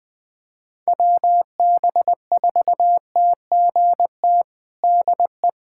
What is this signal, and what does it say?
Morse code, heard as a keyed tone, about 21 wpm, 700 Hz. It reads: WB4TGT DE
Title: USA Callsign DE Callsign - 6 Characters - 20wpm